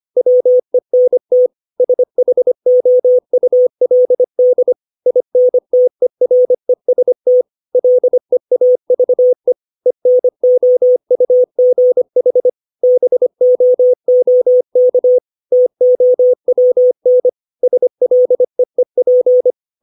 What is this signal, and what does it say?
Morse code, heard as a keyed tone, about 25 wpm, 500 Hz. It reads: WENT SHOULD INTEREST LEAVE ENOUGH BOOK TOWN SLEEP
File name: 632 words / 632 Words 25WPM